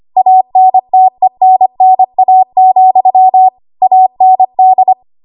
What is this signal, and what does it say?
Morse code, heard as a keyed tone, about 25 wpm, 750 Hz. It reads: ANTENNA, AND
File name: Archive / 25 WPM / 25WPM